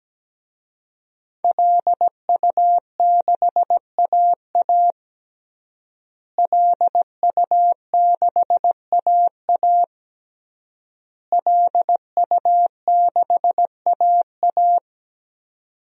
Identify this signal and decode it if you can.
Morse code, heard as a keyed tone, about 17 wpm, 700 Hz. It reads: LU6AA LU6AA LU6AA
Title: DXCC-100 Call Signs - 5 Characters - Repeated 3x - 17wpm